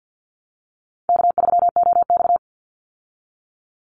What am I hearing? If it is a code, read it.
Morse code, heard as a keyed tone, about 48 wpm, 700 Hz. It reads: K3OX
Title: Easy Call Signs Repeated 3x - 50wpm